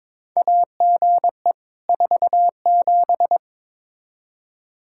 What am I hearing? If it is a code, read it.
Morse code, heard as a keyed tone, about 21 wpm, 700 Hz. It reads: AGE 47
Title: K1USN SST QSO Practice 22wpm